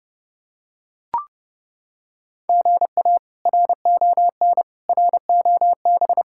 Morse code, heard as a keyed tone, about 30 wpm, 700 Hz. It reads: GA RON ROB
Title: K1USN SST QSO Practice - 30wpm